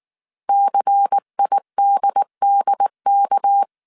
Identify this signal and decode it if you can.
Morse code, heard as a keyed tone, about 19 wpm, 800 Hz. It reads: CIDDK
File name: CW / DET / 1 6 1500C S A 18wpm 2255